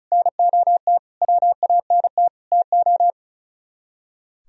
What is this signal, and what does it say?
Morse code, heard as a keyed tone, about 46 wpm, 700 Hz. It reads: NOT WANT TO